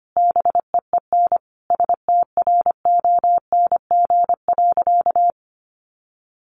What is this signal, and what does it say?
Morse code, heard as a keyed tone, about 25 wpm, 700 Hz. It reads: BEEN STRONG.
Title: Sentences from Top 1400 Words 25wpm